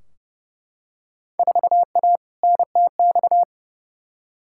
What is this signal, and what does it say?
Morse code, heard as a keyed tone, about 31 wpm, 700 Hz. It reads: 4A NTX
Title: Field Day Exchanges - 30wpm